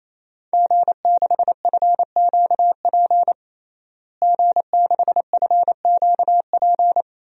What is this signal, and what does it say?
Morse code, heard as a keyed tone, about 28 wpm, 700 Hz. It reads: G6FQP G6FQP
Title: DXCC-100 Call Signs - 5 Characters - Repeated 3x - 28wpm